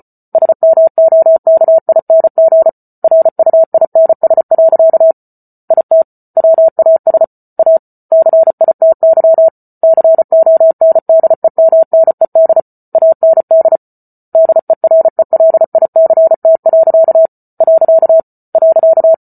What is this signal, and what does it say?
Morse code, heard as a keyed tone, about 34 wpm, 650 Hz. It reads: SMOKING RUINS. IT WAS A CITY CONDEMNED AND DERELICT. .